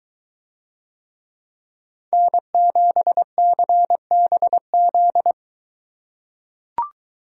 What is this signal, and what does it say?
Morse code, heard as a keyed tone, about 21 wpm, 700 Hz. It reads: N7CBZ